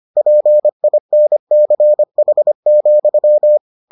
Morse code, heard as a keyed tone, about 25 wpm, 600 Hz. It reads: PINCH,